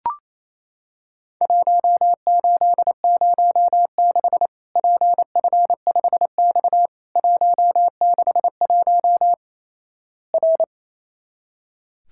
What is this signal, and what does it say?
Morse code, heard as a keyed tone, about 28 wpm, 700 Hz. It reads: 1806 PF5X 161 R
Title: Worked All Europe DX Contest - QTC Exchange - 28wpm